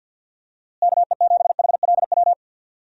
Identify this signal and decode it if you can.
Morse code, heard as a keyed tone, about 50 wpm, 700 Hz. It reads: KE7HLW
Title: USA Call Signs - 6 Characters - Repeated 2x - 50wpm